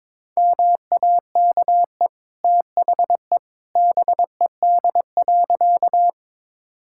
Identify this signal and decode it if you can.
Morse code, heard as a keyed tone, about 21 wpm, 690 Hz. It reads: MAKE THE BED.
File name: practice / 22wpm / Easy Sentences with Hints - 22wpm